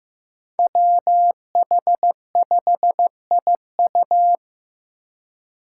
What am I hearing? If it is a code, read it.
Morse code, heard as a keyed tone, about 15 wpm, 700 Hz. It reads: WH5IU
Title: Five Challenging Letter-Number Combinations 15wpm